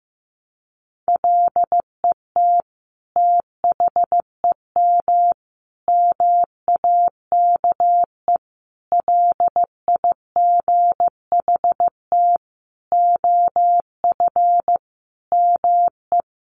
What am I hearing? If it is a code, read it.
Morse code, heard as a keyed tone, about 15 wpm, 700 Hz. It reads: LET THEM MAKE LIGHT OF ME
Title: Sentences from Top 1300 Words 15wpm